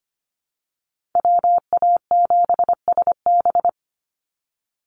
Morse code, heard as a keyed tone, about 25 wpm, 700 Hz. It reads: WA7SB
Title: NA SOTA Chaser Callsigns 25wpm